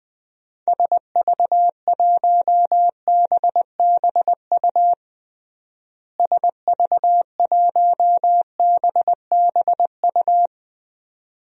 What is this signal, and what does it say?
Morse code, heard as a keyed tone, about 20 wpm, 700 Hz. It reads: SV1BBU SV1BBU
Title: DXCC-100 Call Signs - 6 Characters - Repeated 2x - 20wpm